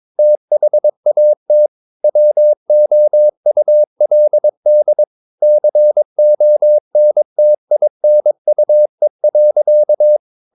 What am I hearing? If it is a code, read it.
Morse code, heard as a keyed tone, about 22 wpm, 600 Hz. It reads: THAT WOULD CONTINUE.
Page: Romanen Never af Ken Follet på engelsk.